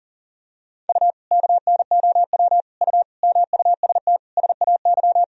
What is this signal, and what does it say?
Morse code, heard as a keyed tone, about 40 wpm, 700 Hz. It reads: U KNOW U MUST SAY
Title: Sentences from Top 500 Words and Abbreviations 40wpm